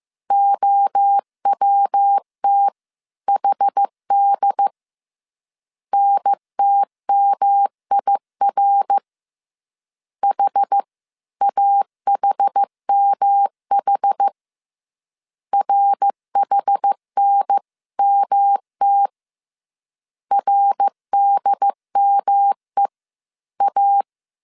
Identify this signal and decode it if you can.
Morse code, heard as a keyed tone, about 15 wpm, 800 Hz. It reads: OWTHD NTMIR HAHMH RHNMT RDMEA